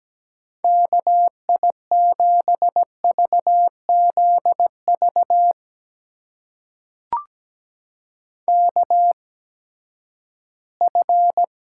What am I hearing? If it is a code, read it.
Morse code, heard as a keyed tone, about 17 wpm, spaced as for 12 wpm, 700 Hz. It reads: KI7VZV KF
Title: USA Call Signs - 6 Characters - 8x Character Spacing - 17wpm